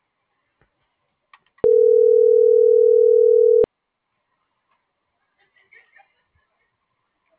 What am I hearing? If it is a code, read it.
Ringback tone (North America), 440 + 480 Hz.
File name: CENTRAL GENERAL / out alma.sky